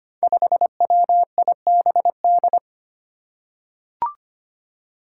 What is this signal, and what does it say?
Morse code, heard as a keyed tone, about 25 wpm, 700 Hz. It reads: 5WIBD
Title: Five Challenging Letter-Number Combinations 25wpm